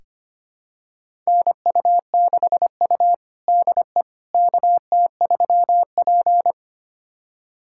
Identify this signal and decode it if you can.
Morse code, heard as a keyed tone, about 25 wpm, 700 Hz. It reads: NU6U DE KT3P
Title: USA Callsign DE Callsign - 4 Characters - 25wpm